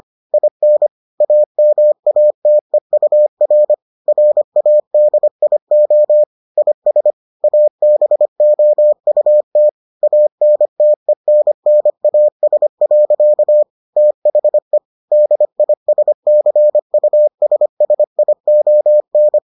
Morse code, heard as a keyed tone, about 25 wpm, 600 Hz. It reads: IN AMATEUR RADIO IS ABOUT ANTENNAS. THE DISCUSSION